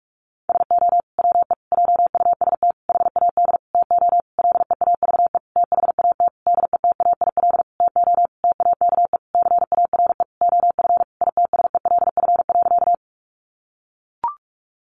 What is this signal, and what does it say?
Morse code, heard as a keyed tone, about 44 wpm, 700 Hz. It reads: SO WE JUST HAD TO LEAVE THAT DETAIL TO TAKE CARE OF ITSELF.
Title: Sentences from Top 1500 Words 45wpm